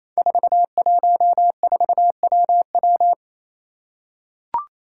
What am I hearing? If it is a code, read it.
Morse code, heard as a keyed tone, about 28 wpm, 700 Hz. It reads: 414WW
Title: Five Challenging Letter-Number Combinations 28wpm